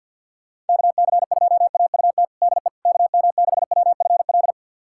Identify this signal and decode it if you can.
Morse code, heard as a keyed tone, about 47 wpm, 700 Hz. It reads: KC2AUT DE KM6WFL
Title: USA Callsign DE Callsign - 6 Characters - 50wpm